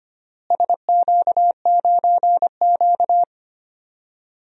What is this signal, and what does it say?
Morse code, heard as a keyed tone, about 25 wpm, 700 Hz. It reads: SQ9Q
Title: DXCC-100 Call Signs - 4 Characters - Repeated 3x - 25wpm